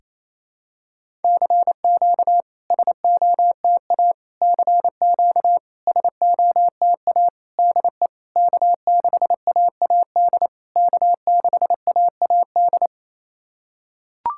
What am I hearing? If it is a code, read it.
Morse code, heard as a keyed tone, about 29 wpm, 660 Hz. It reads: CQ SOTA CQ SOTA DE K6AAD K6AAD
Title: SOTA QSO Practice 28wpm